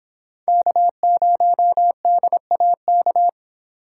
Morse code, heard as a keyed tone, about 26 wpm, 700 Hz. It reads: K0DAK